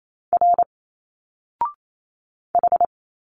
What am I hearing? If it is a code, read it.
Morse code, heard as a keyed tone, about 28 wpm, 700 Hz. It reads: R H
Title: Single Letters 28wpm